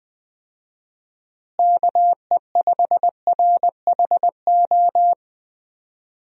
Morse code, heard as a keyed tone, about 20 wpm, 700 Hz. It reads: KE5RHO
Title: USA Call Signs - 6 Characters - Repeated 3x - 20wpm